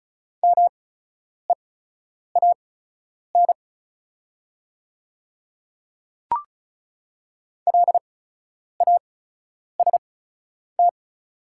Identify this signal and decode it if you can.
Morse code, heard as a keyed tone, about 34 wpm, spaced as for 10 wpm, 700 Hz. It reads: MEAN LAST
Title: 8x Character Spacing - Top 300 Words - 35wpm